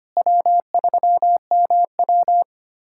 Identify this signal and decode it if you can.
Morse code, heard as a keyed tone, about 25 wpm, 700 Hz. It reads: W3MW
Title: USA Callsign DE Callsign - 4 Characters - 25wpm